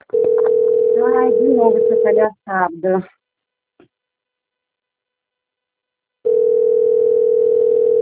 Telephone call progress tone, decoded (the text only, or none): ringback tone